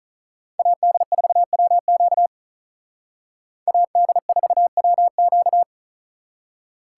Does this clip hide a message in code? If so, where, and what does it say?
Yes: Morse code from 0.6 to 5.6 s, heard as a keyed tone, about 45 wpm, 700 Hz. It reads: AD4WQ AD4WQ